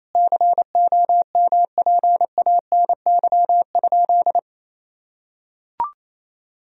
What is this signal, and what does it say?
Morse code, heard as a keyed tone, about 28 wpm, 700 Hz. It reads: COMPANY?
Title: Sentences from Top 1000 Words - Review Part 3 - 28wpm